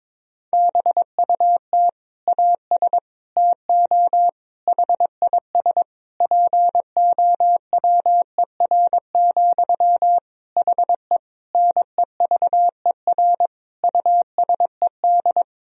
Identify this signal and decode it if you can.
Morse code, heard as a keyed tone, about 22 wpm, 700 Hz. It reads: BUT AS TO HIS POWER, HE NEVER USED
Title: Sentences from Top 1000 Words - Review Part 2 - 22wpm